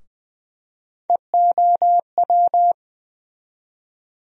Morse code, heard as a keyed tone, about 21 wpm, 700 Hz. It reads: EOW
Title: USA Call Sign Postfixes - 3 Letters 20wpm